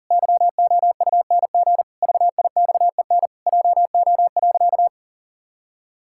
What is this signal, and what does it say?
Morse code, heard as a keyed tone, about 40 wpm, 700 Hz. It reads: YOUNG VIXEN JO.
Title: Pangram Challenge v2 - 40wpm